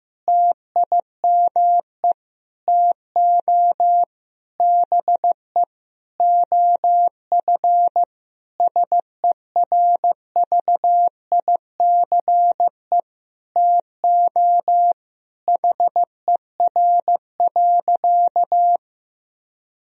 Morse code, heard as a keyed tone, about 15 wpm, 700 Hz. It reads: TIME TO BE OF SERVICE TO HER.
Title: Sentences from Top 1000 Words - Review Part 4 - 15wpm